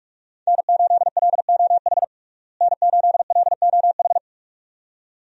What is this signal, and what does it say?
Morse code, heard as a keyed tone, about 46 wpm, 700 Hz. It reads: N8LOH N8LOH
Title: USA Call Signs - 5 Characters - Repeated 2x - 45wpm